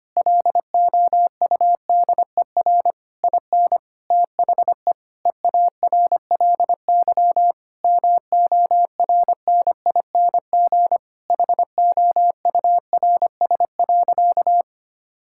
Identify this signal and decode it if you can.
Morse code, heard as a keyed tone, about 25 wpm, 700 Hz. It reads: LOUDER IN THE EARLY MORNING HOURS.
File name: practice / 25wpm / Difficult Sentences with Hints - 25wpm